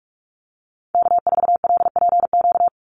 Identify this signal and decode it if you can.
Morse code, heard as a keyed tone, about 44 wpm, 700 Hz. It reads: K4LPQ
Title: NA SOTA Activator Callsigns 45wpm